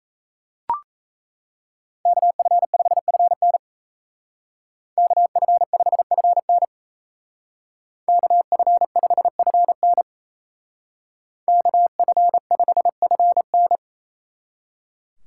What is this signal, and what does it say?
Morse code, heard as a keyed tone, about 36 wpm, 700 Hz. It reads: KF5FN KF5FN KF5FN KF5FN